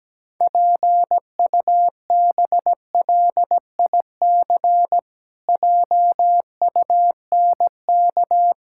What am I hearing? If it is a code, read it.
Morse code, heard as a keyed tone, about 17 wpm, 700 Hz. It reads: PUBLIC JUNK
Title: Pangram Challenge v2 - 17wpm